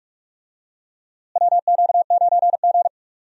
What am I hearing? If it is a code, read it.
Morse code, heard as a keyed tone, about 46 wpm, 700 Hz. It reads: WQ9G
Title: USA Call Signs - 4 Characters - Repeated 3x - 45wpm